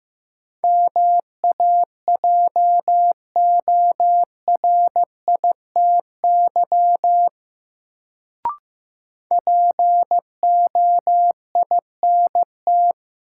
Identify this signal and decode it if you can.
Morse code, heard as a keyed tone, about 15 wpm, 700 Hz. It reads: MAJORITY POINT
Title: Sets of 3 Words - New Word Drill - 15wpm